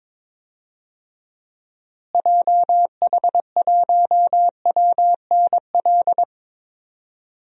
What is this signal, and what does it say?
Morse code, heard as a keyed tone, about 21 wpm, 700 Hz. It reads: JH1WNL
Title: DXCC-100 Call Signs - 6 Characters - Repeated 3x - 22wpm